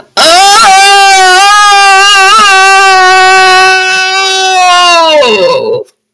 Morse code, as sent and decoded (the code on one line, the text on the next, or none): none